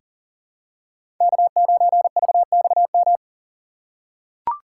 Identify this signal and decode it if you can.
Morse code, heard as a keyed tone, about 40 wpm, 700 Hz. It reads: K9VXM
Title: USA Call Signs - 5 Characters - 2x Character Spacing - 40wpm